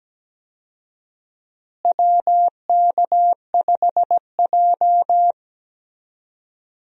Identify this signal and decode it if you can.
Morse code, heard as a keyed tone, about 17 wpm, 700 Hz. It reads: WK5J
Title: USA Call Signs - 4 Characters - Repeated 2x - 17wpm